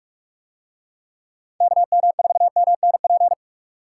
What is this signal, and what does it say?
Morse code, heard as a keyed tone, about 44 wpm, 700 Hz. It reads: KM4MNP